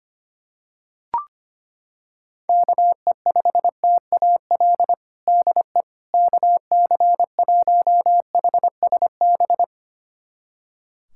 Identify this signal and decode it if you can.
Morse code, heard as a keyed tone, about 25 wpm, 700 Hz. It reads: KE5TAL DE KC1HSB
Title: USA Callsign DE Callsign - 6 Characters - 25wpm